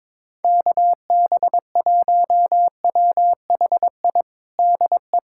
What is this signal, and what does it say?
Morse code, heard as a keyed tone, about 22 wpm, 700 Hz. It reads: KB1WHI DE
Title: USA Callsign DE Callsign - 6 Characters - 22wpm